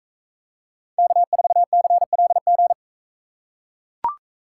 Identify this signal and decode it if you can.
Morse code, heard as a keyed tone, about 42 wpm, 700 Hz. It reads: K4CLG